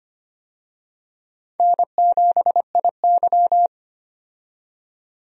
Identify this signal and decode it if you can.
Morse code, heard as a keyed tone, about 25 wpm, 700 Hz. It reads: N7IY
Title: USA Call Signs - 4 Characters - 25wpm